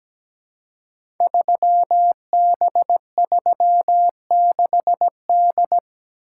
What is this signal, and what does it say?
Morse code, heard as a keyed tone, about 17 wpm, 700 Hz. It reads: 3B36D
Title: Code Groups - 5 Letter:Numbers - 17wpm